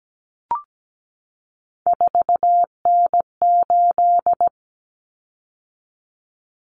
Morse code, heard as a keyed tone, about 17 wpm, 690 Hz. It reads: 4N8
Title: Common Cut Numbers - 3-Digit 17wpm